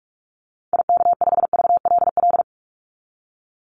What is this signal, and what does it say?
Morse code, heard as a keyed tone, about 44 wpm, 700 Hz. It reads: IK5VLL
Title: Call Signs - 45wpm